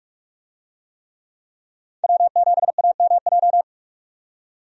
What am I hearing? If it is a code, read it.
Morse code, heard as a keyed tone, about 44 wpm, 700 Hz. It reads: W7AMJ
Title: USA Call Signs - 5 Characters - 45wpm